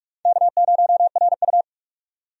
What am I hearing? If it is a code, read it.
Morse code, heard as a keyed tone, about 46 wpm, 700 Hz. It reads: K0RU
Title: USA Call Signs - 4 Characters - 45wpm